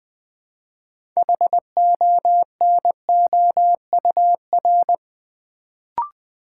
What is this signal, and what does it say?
Morse code, heard as a keyed tone, about 20 wpm, 700 Hz. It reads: HONOUR
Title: Build A Word - 3 Letters At A Time - 6-Letter Words - 20wpm